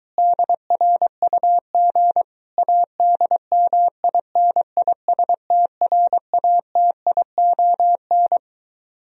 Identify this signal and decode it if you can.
Morse code, heard as a keyed tone, about 22 wpm, 700 Hz. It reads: DRUG ADMINISTRATION